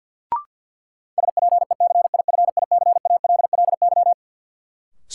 Morse code, heard as a keyed tone, about 50 wpm, 700 Hz. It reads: SPECIFICALLY
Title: Sentences from Top 1100 Words - New Word Drill - 50wpm